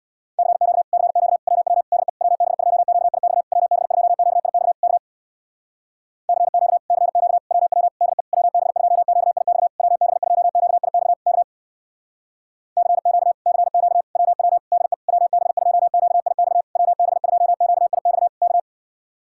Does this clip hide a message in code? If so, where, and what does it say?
Yes: Morse code from 0.4 to 18.6 s, heard as a keyed tone, about 67 wpm, 700 Hz. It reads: CQ CQ WW DE WB2QIX WB2QIX K CQ CQ WW DE WB2QIX WB2QIX K CQ CQ WW DE WB2QIX WB2QIX K